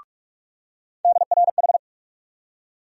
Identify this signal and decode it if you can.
Morse code, heard as a keyed tone, about 46 wpm, 700 Hz. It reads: DRH
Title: USA Call Sign Postfixes - 3 Letters 45wpm